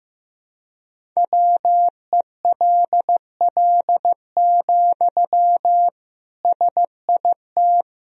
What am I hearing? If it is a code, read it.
Morse code, heard as a keyed tone, about 15 wpm, 700 Hz. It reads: WELL, SIT
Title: Sentences from Top 500 Words and Abbreviations 15wpm